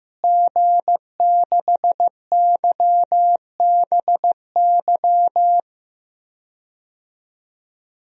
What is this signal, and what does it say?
Morse code, heard as a keyed tone, about 15 wpm, 700 Hz. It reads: G6YBY
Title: DXCC-100 Call Signs - 5 Characters - Repeated 2x - 15wpm